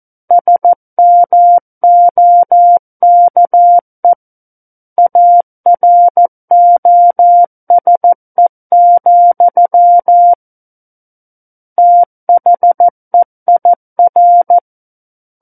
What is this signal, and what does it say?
Morse code, heard as a keyed tone, about 14 wpm, 700 Hz. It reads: SMOKE AROSE, THEIR
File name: Book One / 05-Chapter 5 - 14 WPM